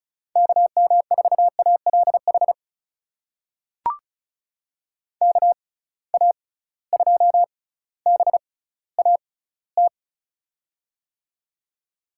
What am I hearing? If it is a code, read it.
Morse code, heard as a keyed tone, about 36 wpm, spaced as for 18 wpm, 700 Hz. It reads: KM4ALH KA2BAT
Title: USA Call Signs - 6 Characters - 6x Character Spacing - 35wpm